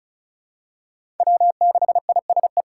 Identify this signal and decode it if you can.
Morse code, heard as a keyed tone, about 36 wpm, 700 Hz. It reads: W6ISE
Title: USA Call Signs - 5 Characters - 6x Character Spacing - 35wpm